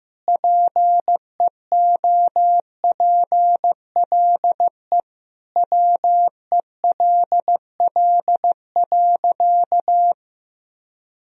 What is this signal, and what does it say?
Morse code, heard as a keyed tone, about 15 wpm, 700 Hz. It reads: PEOPLE WELL.
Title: Sentences from Top 1000 Words - Review Part 2 - 15wpm